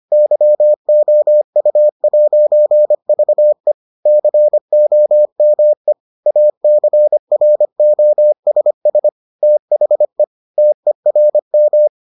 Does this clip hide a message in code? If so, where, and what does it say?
Yes: Morse code from 0.1 to 11.9 s, heard as a keyed tone, about 25 wpm, 600 Hz. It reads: YOU'VE COME ACROSS THE TERM